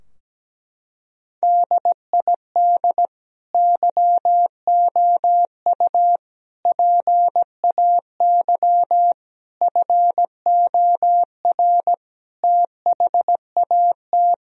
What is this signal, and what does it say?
Morse code, heard as a keyed tone, about 17 wpm, 700 Hz. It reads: DID YOU PAY FOR THAT
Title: Sentences from Top 1000 Words - Review Part 3 - 17wpm